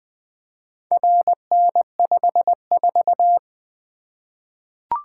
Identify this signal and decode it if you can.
Morse code, heard as a keyed tone, about 20 wpm, 700 Hz. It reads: RN54
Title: Four Letter-Number Combinations - 20wpm